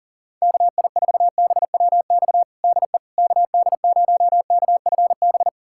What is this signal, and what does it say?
Morse code, heard as a keyed tone, about 40 wpm, 690 Hz. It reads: KI4BWX DE KD0KFB
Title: USA Callsign DE Callsign - 6 Characters - 40wpm